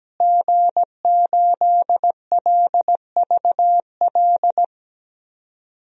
Morse code, heard as a keyed tone, about 17 wpm, 700 Hz. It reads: G8LVL
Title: DXCC-100 Call Signs - 5 Characters - Repeated 3x - 17wpm